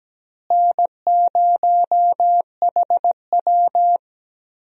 Morse code, heard as a keyed tone, about 17 wpm, 700 Hz. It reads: N0HW
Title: USA Call Signs - 4 Characters - Repeated 2x - 17wpm